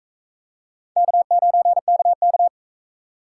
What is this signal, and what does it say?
Morse code, heard as a keyed tone, about 44 wpm, 700 Hz. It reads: K9KK